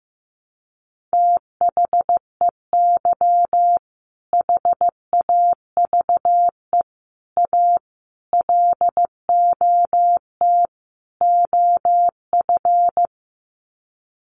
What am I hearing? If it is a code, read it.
Morse code, heard as a keyed tone, about 15 wpm, 700 Hz. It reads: THEY HAVE A LOT OF
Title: Sets of 5 Words Encore2 15wpm